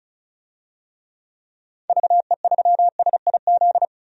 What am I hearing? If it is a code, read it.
Morse code, heard as a keyed tone, about 34 wpm, 700 Hz. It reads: VE3SIZ
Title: DXCC-100 Call Signs - 6 Characters - 35wpm